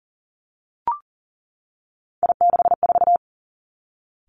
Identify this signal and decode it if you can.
Morse code, heard as a keyed tone, about 40 wpm, 700 Hz. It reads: I64
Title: Three Challenging Letter-Number Combinations 40wpm